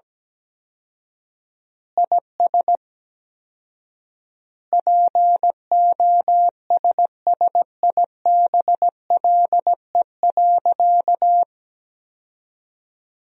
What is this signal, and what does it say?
Morse code, heard as a keyed tone, about 17 wpm, 700 Hz. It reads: IS POSSIBLE.
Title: 4x Word-Spacing - Sentences from Top 500 Words - 17wpm